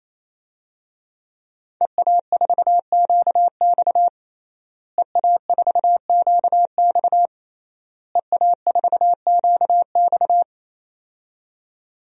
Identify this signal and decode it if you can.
Morse code, heard as a keyed tone, about 28 wpm, 700 Hz. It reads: EA4QX EA4QX EA4QX
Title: DXCC-100 Call Signs - 5 Characters - Repeated 3x - 28wpm